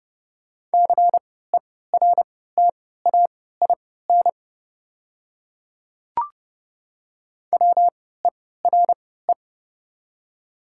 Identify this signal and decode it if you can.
Morse code, heard as a keyed tone, about 30 wpm, spaced as for 17 wpm, 700 Hz. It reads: CERTAIN WERE
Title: 3x Character Spacing - Top 500 Words - 30wpm